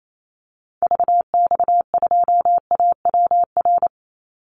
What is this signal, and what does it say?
Morse code, heard as a keyed tone, about 28 wpm, 700 Hz. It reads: VX2AWR
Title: Call Signs Repeated 2x - 28wpm